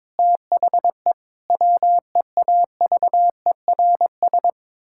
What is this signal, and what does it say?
Morse code, heard as a keyed tone, about 22 wpm, 700 Hz. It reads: THE WEAVERS
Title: Pangram Challenge v2 - 22wpm